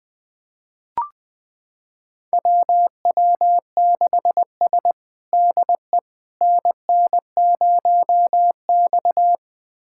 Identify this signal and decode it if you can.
Morse code, heard as a keyed tone, about 20 wpm, 700 Hz. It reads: WW6S DE NN0X
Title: USA Callsign DE Callsign - 4 Characters - 20wpm